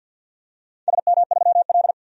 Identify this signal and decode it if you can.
Morse code, heard as a keyed tone, about 53 wpm, 700 Hz. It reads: SM3L
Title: DXCC-100 Call Signs - 4 Characters - 50wpm